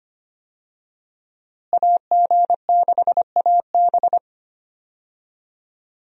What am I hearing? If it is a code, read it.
Morse code, heard as a keyed tone, about 25 wpm, 700 Hz. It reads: AG6AB
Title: USA Call Signs - 5 Characters - Repeated 2x - 25wpm